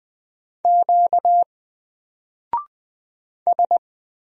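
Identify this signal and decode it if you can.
Morse code, heard as a keyed tone, about 20 wpm, 700 Hz. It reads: Q S
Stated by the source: Single letter and number Morse drill at 20 wpm, from N4LCD.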